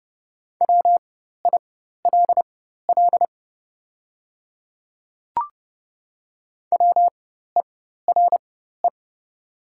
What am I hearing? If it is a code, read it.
Morse code, heard as a keyed tone, about 30 wpm, spaced as for 14 wpm, 690 Hz. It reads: WILL WERE
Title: 4x Character Spacing - Top 100 Words - 30wpm